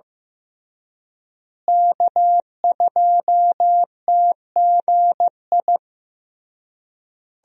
Morse code, heard as a keyed tone, about 16 wpm, 700 Hz. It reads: K2TGI
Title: USA Call Signs - 5 Characters - Repeated 3x - 15wpm